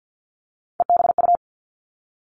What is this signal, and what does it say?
Morse code, heard as a keyed tone, about 50 wpm, 700 Hz. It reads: EBU
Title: All Cut Numbers - 3-Digit 50wpm